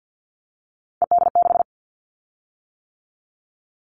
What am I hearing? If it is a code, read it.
Morse code, heard as a keyed tone, about 50 wpm, 700 Hz. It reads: ED6
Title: Callsign Prefixes for Top 50 DXCC - 50wpm